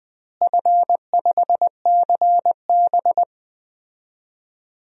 Morse code, heard as a keyed tone, about 20 wpm, 700 Hz. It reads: F5CB
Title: DXCC-100 Call Signs - 4 Characters - Repeated 2x - 20wpm